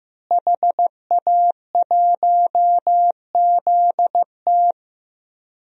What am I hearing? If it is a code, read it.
Morse code, heard as a keyed tone, about 15 wpm, 700 Hz. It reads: HA1ZT
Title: DXCC-100 Call Signs - 5 Characters - Repeated 2x - 15wpm